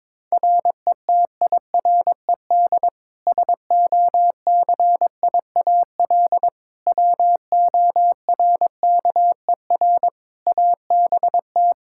Morse code, heard as a keyed tone, about 22 wpm, 700 Hz. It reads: RETIRED SOCIAL WORKER ABT
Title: K1USN SST QSO Practice 22wpm